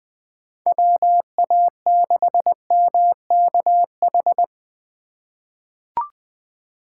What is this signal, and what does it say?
Morse code, heard as a keyed tone, about 20 wpm, 700 Hz. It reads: WA6MKH
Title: USA Call Signs - 6 Characters - 2x Character Spacing - 20wpm